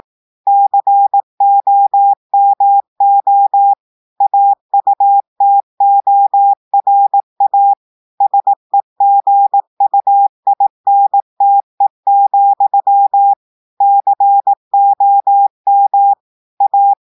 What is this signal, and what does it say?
Morse code, heard as a keyed tone, about 18 wpm, 800 Hz. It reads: COMO AUTORA SEGUINTE, COM A
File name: audios / portugais / portugais texte en clair 1 vitesse 18 mots minutes